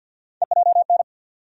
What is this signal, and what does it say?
Morse code, heard as a keyed tone, about 49 wpm, 700 Hz. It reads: EJN
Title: USA Call Sign Postfixes - 3 Letters 50wpm